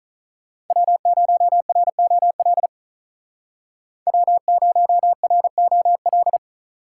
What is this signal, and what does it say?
Morse code, heard as a keyed tone, about 46 wpm, 700 Hz. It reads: W0ROL W0ROL